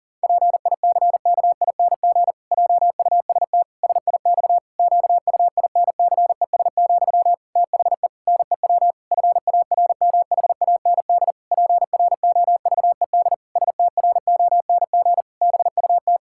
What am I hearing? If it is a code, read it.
Morse code, heard as a keyed tone, about 40 wpm, 700 Hz. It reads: PICKING JUST SIX QUINCES, THE NEW FARMHAND PROVED STRONG BUT
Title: Pangram Challenge v2 - 40wpm